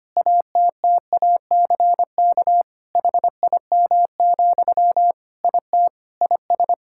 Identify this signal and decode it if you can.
Morse code, heard as a keyed tone, about 25 wpm, 700 Hz. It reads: ATTACK HIM, IT IS
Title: Sentences from Top 1600 Words 25wpm